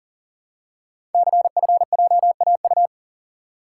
Morse code, heard as a keyed tone, about 39 wpm, 690 Hz. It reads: CFJAU
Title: Code Groups - 5 Letters - 40wpm